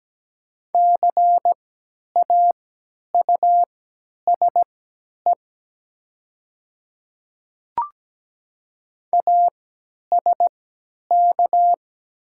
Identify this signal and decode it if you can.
Morse code, heard as a keyed tone, about 17 wpm, spaced as for 10 wpm, 700 Hz. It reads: CAUSE ASK
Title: 3x Character Spacing - Top 500 Words - 17wpm